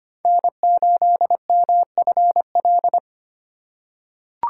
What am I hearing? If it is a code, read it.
Morse code, heard as a keyed tone, about 25 wpm, 700 Hz. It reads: N8MFL
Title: USA Call Signs - 5 Characters - 2x Character Spacing - 25wpm